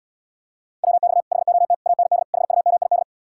Morse code, heard as a keyed tone, about 77 wpm, 700 Hz. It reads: WX HOT AND HUMID